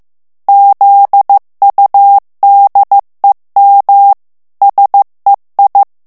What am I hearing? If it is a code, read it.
Morse code, heard as a keyed tone, about 15 wpm, 800 Hz. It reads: ZUDEM SEI
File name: morse mp3 / ALLEMAND texte clair 1 vitesse 14 mots minute